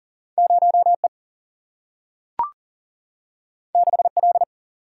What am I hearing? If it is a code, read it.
Morse code, heard as a keyed tone, about 40 wpm, 700 Hz. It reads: 0E 6L
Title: Single-Letter Single-Number 40wpm